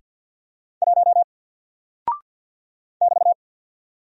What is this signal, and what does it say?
Morse code, heard as a keyed tone, about 48 wpm, 700 Hz. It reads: 1 <BT>
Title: Single Character (Letter:Number:Punctuation:Prosign) Alt 50wpm